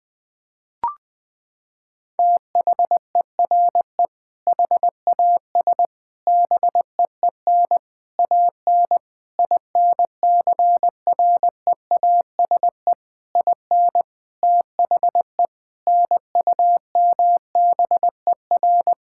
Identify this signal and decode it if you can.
Morse code, heard as a keyed tone, about 21 wpm, 700 Hz. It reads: THERE HAS BEEN AN INCREASE IN THE NUMBER
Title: Sentences with 6-Word Phrases - 20wpm